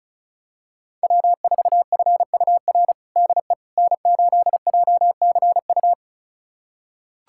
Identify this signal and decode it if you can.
Morse code, heard as a keyed tone, about 34 wpm, 700 Hz. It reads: W4FUR DE N8JCU
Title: USA Callsign DE Callsign - 5 Characters - 35wpm